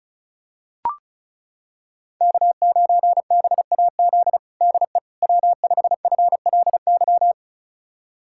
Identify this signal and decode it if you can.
Morse code, heard as a keyed tone, about 36 wpm, 700 Hz. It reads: K9BAZ DE W5FLY
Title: USA Callsign DE Callsign - 5 Characters - 35wpm